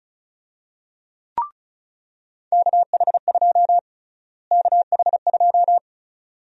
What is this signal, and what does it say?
Morse code, heard as a keyed tone, about 34 wpm, 700 Hz. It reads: KH2 KH2
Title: USA Call Sign Prefixes - Repeated 2x - 35wpm